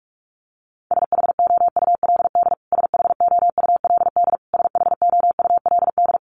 Morse code, heard as a keyed tone, about 44 wpm, 700 Hz. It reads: SHOULD SHOULD SHOULD
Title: Top 300 Words Repeated 45wpm